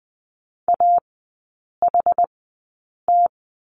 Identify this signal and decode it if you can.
Morse code, heard as a keyed tone, about 20 wpm, spaced as for 8 wpm, 690 Hz. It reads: AHT
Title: 060 Lesson 11e - Letter h - Three Characters 20wpm